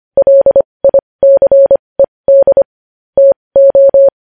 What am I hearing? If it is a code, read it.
Morse code, heard as a keyed tone, about 25 wpm, spaced as for 20 wpm, 550 Hz. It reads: LICED TO
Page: News Headlines in Morse Code at 20 WPM